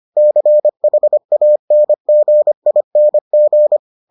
Morse code, heard as a keyed tone, about 25 wpm, 600 Hz. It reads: CHANGING